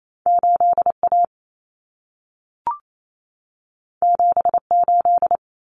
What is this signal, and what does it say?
Morse code, heard as a keyed tone, about 28 wpm, 700 Hz. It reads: 8A 78
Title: Common Cut Numbers - 2-Digit 28wpm